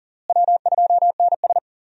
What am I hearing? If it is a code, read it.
Morse code, heard as a keyed tone, about 40 wpm, 700 Hz. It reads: W2NS
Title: USA Call Signs - 4 Characters - Repeated 3x - 40wpm